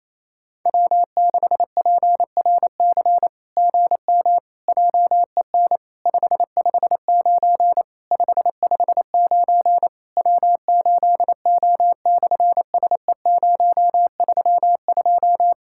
Morse code, heard as a keyed tone, about 27 wpm, 700 Hz. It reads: W6PRC GM JEN 559 559 W8O/SE032
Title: SOTA QSO Practice 28wpm